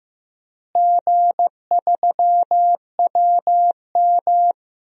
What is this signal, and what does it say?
Morse code, heard as a keyed tone, about 15 wpm, 700 Hz. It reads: G3WM
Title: DXCC-100 Call Signs - 4 Characters - Repeated 2x - 15wpm